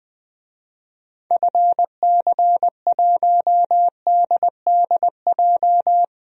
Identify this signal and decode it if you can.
Morse code, heard as a keyed tone, about 20 wpm, 700 Hz. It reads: FC1DDJ
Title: DXCC-100 Call Signs - 6 Characters - 20wpm